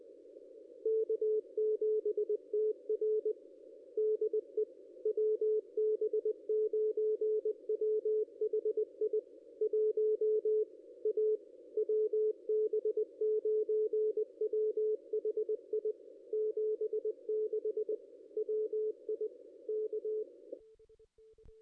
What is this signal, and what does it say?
Morse code, heard as a keyed tone, about 20 wpm, 430 Hz. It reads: K7TR DE WB9WHI 1 A WB9WHI 76 WI K
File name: contests / ss-cw-k5tr-op-n5rz -1900